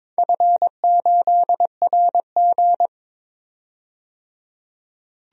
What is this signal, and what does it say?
Morse code, heard as a keyed tone, about 22 wpm, 690 Hz. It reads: F8RG
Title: DXCC-100 Call Signs - 4 Characters - 22wpm